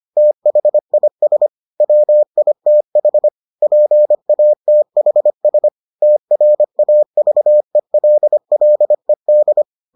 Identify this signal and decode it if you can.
Morse code, heard as a keyed tone, about 25 wpm, 600 Hz. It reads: THIS WITH PATHS TRAVELLED